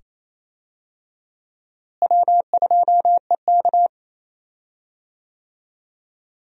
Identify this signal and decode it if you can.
Morse code, heard as a keyed tone, about 28 wpm, 700 Hz. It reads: W2EK
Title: USA Call Signs - 4 Characters - Repeated 2x - 28wpm